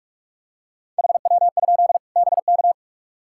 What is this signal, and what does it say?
Morse code, heard as a keyed tone, about 43 wpm, 700 Hz. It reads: HW? BK
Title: K1USN SST QSO Practice 45wpm